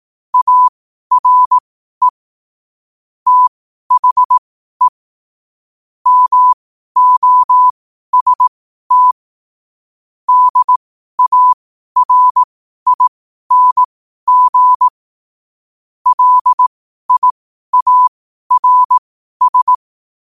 Morse code, heard as a keyed tone, about 18 wpm, spaced as for 12 wpm, 1000 Hz. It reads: ARE THE MOST DARING LIARS
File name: qotdfiles / 12wpm